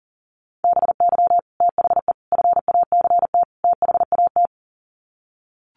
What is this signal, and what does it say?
Morse code, heard as a keyed tone, about 39 wpm, 700 Hz. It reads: BY THE FACT THAT
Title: Sets of 4 Words 40wpm